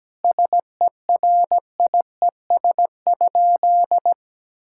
Morse code, heard as a keyed tone, about 17 wpm, 700 Hz. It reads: SERIES?
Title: Sentences from Top 1000 Words - Review Part 5 - 17wpm